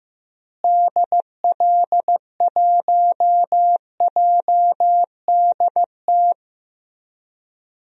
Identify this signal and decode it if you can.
Morse code, heard as a keyed tone, about 15 wpm, 700 Hz. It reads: DL1JDT
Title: DXCC-100 Call Signs - 6 Characters - Repeated 3x - 15wpm